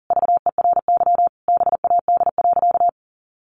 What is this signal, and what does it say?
Morse code, heard as a keyed tone, about 40 wpm, 700 Hz. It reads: VERY BAD.
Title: Sentences from Top 700 Words 40wpm